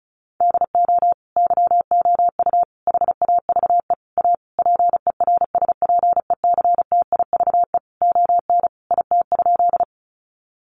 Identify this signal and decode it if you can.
Morse code, heard as a keyed tone, about 34 wpm, 700 Hz. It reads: DO YOU HAVE A PERSPECTIVE ON IT?
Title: Sentences from Top 1200 Words 35wpm